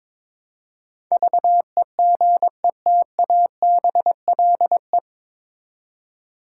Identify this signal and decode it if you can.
Morse code, heard as a keyed tone, about 21 wpm, 700 Hz. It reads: VEGETABLE
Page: Morse Code Ninja practice 22wpm